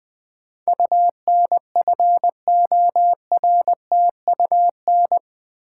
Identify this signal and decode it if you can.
Morse code, heard as a keyed tone, about 20 wpm, 700 Hz. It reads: UNFORTUN
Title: Build A Word - 4 Letters At A Time - 11-Letter Words - 20wpm